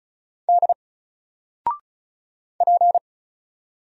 Morse code, heard as a keyed tone, about 33 wpm, 700 Hz. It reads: D P
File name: Single-Letter-Number-35wpm